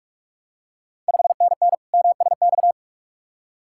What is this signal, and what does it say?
Morse code, heard as a keyed tone, about 46 wpm, 690 Hz. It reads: 5NN MSX
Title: Ontario QSO Party Exchange 45wpm